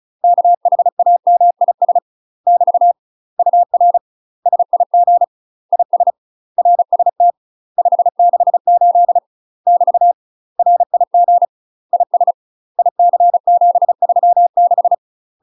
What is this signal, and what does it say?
Morse code, heard as a keyed tone, about 34 wpm, 700 Hz. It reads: KHAMIS <BT> UR SIG IS RST 568 <BT> RIG IS IC736
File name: UK MASTER QSOs TRAINING / 35WPM / UK MASTER QSOs 0037